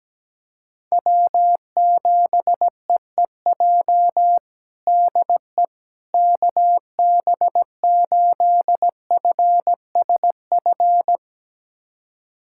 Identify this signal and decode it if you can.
Morse code, heard as a keyed tone, about 17 wpm, 700 Hz. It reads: W7EEJ DE KB8FSF
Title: USA Callsign DE Callsign 17wpm